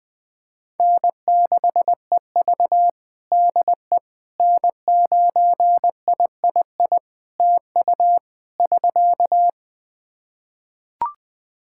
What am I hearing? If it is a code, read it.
Morse code, heard as a keyed tone, about 20 wpm, 700 Hz. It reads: N6EV DE N9III TU <SK>
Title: Practice QSO - Exchange 4 - Ending - 20wpm